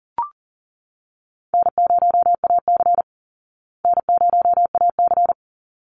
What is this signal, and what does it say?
Morse code, heard as a keyed tone, about 38 wpm, 700 Hz. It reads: N0AC N0AC
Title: Call Signs Repeated 3x - 40wpm